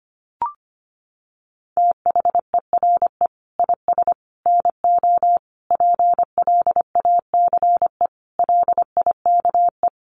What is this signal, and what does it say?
Morse code, heard as a keyed tone, about 25 wpm, 700 Hz. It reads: THERE IS NO PLACE LIKE
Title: Sentences from Top 200 Words 25wpm